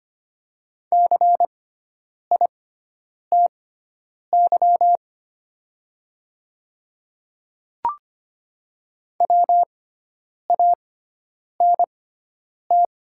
Morse code, heard as a keyed tone, about 25 wpm, spaced as for 9 wpm, 700 Hz. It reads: CITY WANT
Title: 6x Character Spacing - Top 300 Words - 25wpm